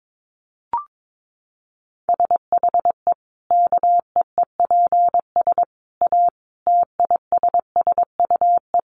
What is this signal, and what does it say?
Morse code, heard as a keyed tone, about 22 wpm, 700 Hz. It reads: SHE KEEPS A TISSUE
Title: Easy Phrases 22wpm